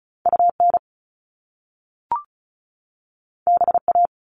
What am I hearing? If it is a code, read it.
Morse code, heard as a keyed tone, about 36 wpm, 700 Hz. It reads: UN BA
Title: Top Two Letter Combinations in Words 35wpm